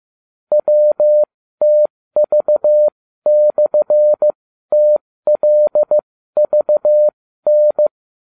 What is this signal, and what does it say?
Morse code, heard as a keyed tone, about 15 wpm, spaced as for 13 wpm, 600 Hz. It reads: WTV/TLVN